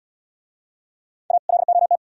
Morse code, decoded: A LOT

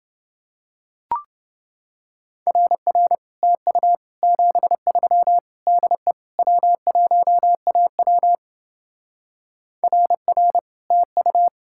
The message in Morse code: RR TU 73 DE W1AW RR TU

